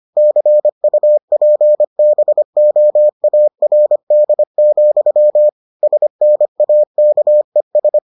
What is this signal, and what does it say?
Morse code, heard as a keyed tone, about 25 wpm, 600 Hz. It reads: CUPBOARD, SNAKES